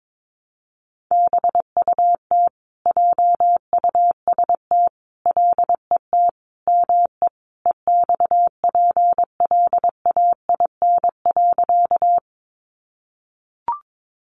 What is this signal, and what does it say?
Morse code, heard as a keyed tone, about 22 wpm, 700 Hz. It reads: BUT JUST LET ME EXPLAIN.
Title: Sentences from Top 1100 Words 22wpm